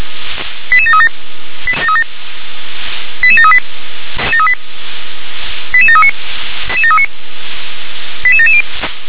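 Five-tone selective call (ZVEI), heard as quick runs of five tones, tones of about 70 ms. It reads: 88737 88738 88739 88770
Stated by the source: ZVEI five-tone selective call, address 88665, received on an SDR.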